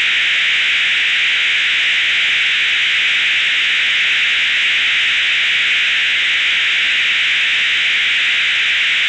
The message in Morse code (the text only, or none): none